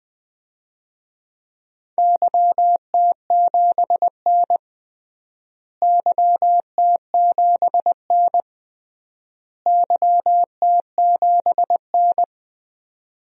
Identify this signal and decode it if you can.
Morse code, heard as a keyed tone, about 21 wpm, 700 Hz. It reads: YT7N YT7N YT7N
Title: DXCC-100 Call Signs - 4 Characters - Repeated 3x - 20wpm